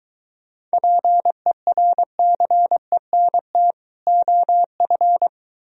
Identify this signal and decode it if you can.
Morse code, heard as a keyed tone, about 23 wpm, 700 Hz. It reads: PERCENT OF